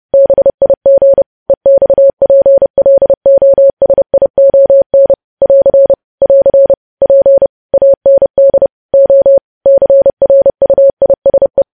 Morse code, heard as a keyed tone, about 30 wpm, 550 Hz. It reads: BIG EXPLOSION <AR> <AR> P AND O CRUISE
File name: podcast / News headlines-1x-30x30